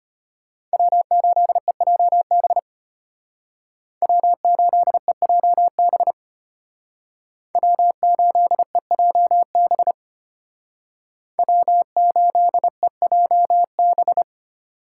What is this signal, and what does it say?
Morse code, heard as a keyed tone, about 30 wpm, 700 Hz. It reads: W8EJB W8EJB W8EJB W8EJB
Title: Speed Racing - USA 1x3 Call Signs 25wpm